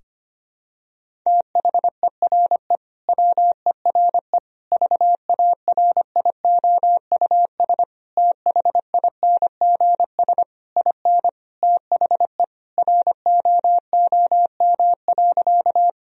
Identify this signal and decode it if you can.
Morse code, heard as a keyed tone, about 25 wpm, 700 Hz. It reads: THERE WERE VARIOUS THINGS IN THE ROOM.
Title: Sentences from Top 1000 Words - Review Part 4 - 25wpm